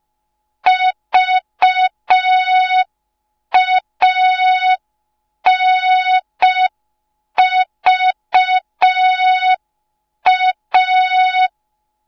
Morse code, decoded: VANVA